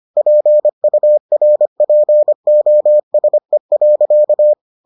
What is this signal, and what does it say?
Morse code, heard as a keyed tone, about 25 wpm, 600 Hz. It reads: PURPOSE.